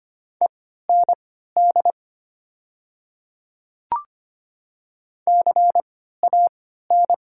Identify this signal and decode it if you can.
Morse code, heard as a keyed tone, about 25 wpm, spaced as for 14 wpm, 700 Hz. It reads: END CAN
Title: 3x Character Spacing - Top 200 Words - 25wpm